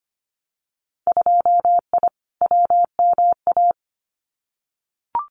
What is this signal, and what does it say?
Morse code, heard as a keyed tone, about 25 wpm, 700 Hz. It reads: 2I WMA
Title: Winter Field Day Exchanges Encore 25wpm